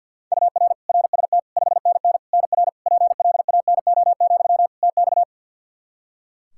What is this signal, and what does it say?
Morse code, heard as a keyed tone, about 48 wpm, 700 Hz. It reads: UR RST 5NN NR PLANO, TX
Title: Practice QSO - Exchange 1 - Beginning - 50wpm